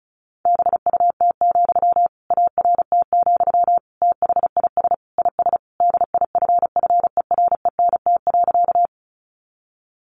Morse code, heard as a keyed tone, about 34 wpm, 700 Hz. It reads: BUT, ART, THIS IS DIFFERENT.
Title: Sentences from Top 600 Words 35wpm